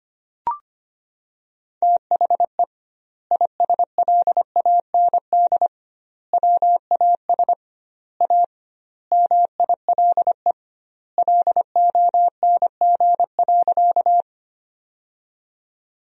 Morse code, heard as a keyed tone, about 25 wpm, 700 Hz. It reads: THE ISLAND WAS A MILE LONG.
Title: 2x Word-Spacing - Sentences from Top 500 Words - 25wpm